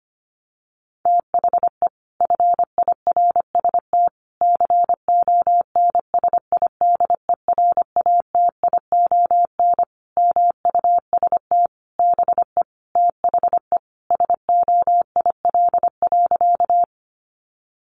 Morse code, heard as a keyed tone, about 25 wpm, 700 Hz. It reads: THE FIRST CONSIDERATION MUST BE THE SOIL.
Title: Sentences from Top 1500 Words 25wpm